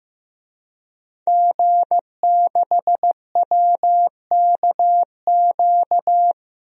Morse code, heard as a keyed tone, about 15 wpm, 700 Hz. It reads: G6WKQ
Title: DXCC-100 Call Signs - 5 Characters - 15wpm